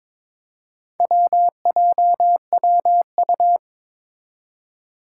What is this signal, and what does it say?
Morse code, heard as a keyed tone, about 22 wpm, 700 Hz. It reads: WJWU
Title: Four Challenging Letter-Number Combinations 22wpm